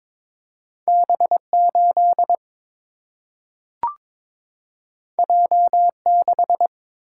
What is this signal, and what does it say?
Morse code, heard as a keyed tone, about 23 wpm, 700 Hz. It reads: B 8 J 6
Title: Single-Letter Single-Number 22wpm